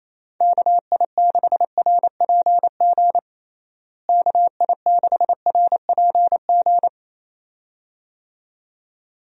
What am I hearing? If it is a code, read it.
Morse code, heard as a keyed tone, about 27 wpm, 700 Hz. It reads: KI6RPG KI6RPG
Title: USA Call Signs - 6 Characters - Repeated 2x - 28wpm